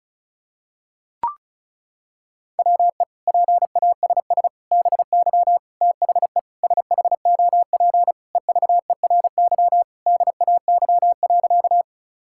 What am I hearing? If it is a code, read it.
Morse code, heard as a keyed tone, about 34 wpm, 700 Hz. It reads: WE PASS BY THE SHOP EVERY DAY.
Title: Sentences from Top 1600 Words 35wpm